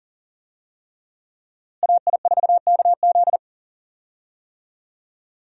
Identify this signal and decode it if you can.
Morse code, heard as a keyed tone, about 40 wpm, 700 Hz. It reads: AI4KZ
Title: USA Call Signs - 5 Characters - Repeated 3x - 40wpm